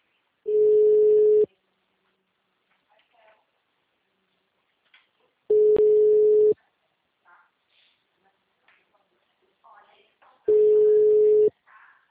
Ringback tone (Europe, 425 Hz).
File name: out- -2020- - - .2947